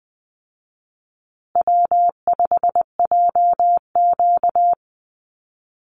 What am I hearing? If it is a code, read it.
Morse code, heard as a keyed tone, about 20 wpm, 700 Hz. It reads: W5JQ
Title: Easy Call Signs Repeated 2x - 20wpm